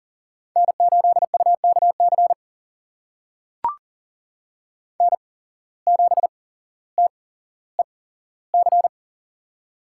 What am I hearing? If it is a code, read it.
Morse code, heard as a keyed tone, about 40 wpm, spaced as for 17 wpm, 700 Hz. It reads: N8UKC N7TEC
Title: USA Call Signs - 8x Character Spacing - 40wpm